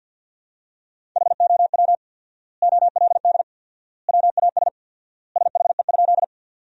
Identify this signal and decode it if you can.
Morse code, heard as a keyed tone, about 52 wpm, 700 Hz. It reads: HOW OLD WAS SHE?
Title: 4x Word-Spacing - Sentences from Top 200 Words - 50wpm